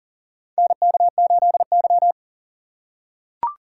Morse code, heard as a keyed tone, about 40 wpm, 700 Hz. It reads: NK8Y
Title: USA Call Signs - 4 Characters - 6x Character Spacing - 40wpm